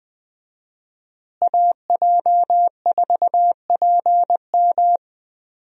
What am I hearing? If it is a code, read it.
Morse code, heard as a keyed tone, about 20 wpm, 700 Hz. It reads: AJ4PM
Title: USA Call Signs - 5 Characters - Repeated 3x - 20wpm